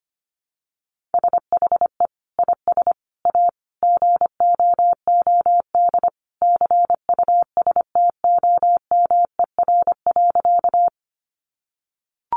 Morse code, heard as a keyed tone, about 25 wpm, 700 Hz. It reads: SHE IS A GOOD CUSTOMER.
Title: Sentences from Top 1100 Words 25wpm